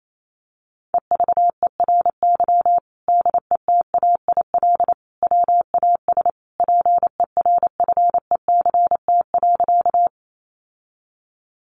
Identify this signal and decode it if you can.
Morse code, heard as a keyed tone, about 28 wpm, 700 Hz. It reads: EVERY DETAIL WAS PERFECT.
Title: Sentences from Top 1500 Words 28wpm